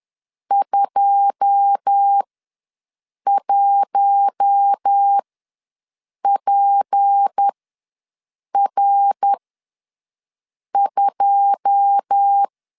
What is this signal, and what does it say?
Morse code, heard as a keyed tone, about 10 wpm, spaced as for 6 wpm, 800 Hz. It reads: 21PR2
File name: LEZIONI / 9 2 ws 10wpm 4130